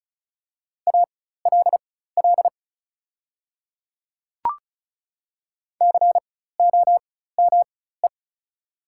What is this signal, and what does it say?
Morse code, heard as a keyed tone, about 35 wpm, spaced as for 16 wpm, 690 Hz. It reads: ALL COME
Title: 4x Character Spacing - Top 100 Words - 35wpm